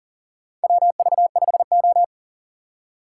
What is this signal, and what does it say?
Morse code, heard as a keyed tone, about 40 wpm, 700 Hz. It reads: WV5O
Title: USA Call Signs - 4 Characters - 4x Character Spacing - 40wpm